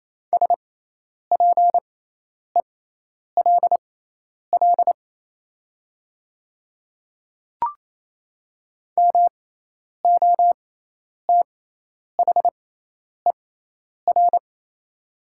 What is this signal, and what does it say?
Morse code, heard as a keyed tone, about 28 wpm, spaced as for 10 wpm, 700 Hz. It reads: SPELL MOTHER
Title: 6x Character Spacing - Top 200 Words - 28wpm